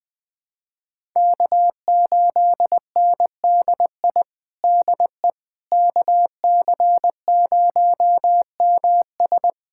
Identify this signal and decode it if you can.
Morse code, heard as a keyed tone, about 20 wpm, 700 Hz. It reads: K8NDI DE KC0MS
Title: USA Callsign DE Callsign - 5 Characters - 20wpm